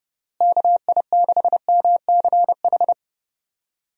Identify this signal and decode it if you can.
Morse code, heard as a keyed tone, about 30 wpm, 700 Hz. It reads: KI6MCH
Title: USA Call Signs - 6 Characters - Repeated 2x - 30wpm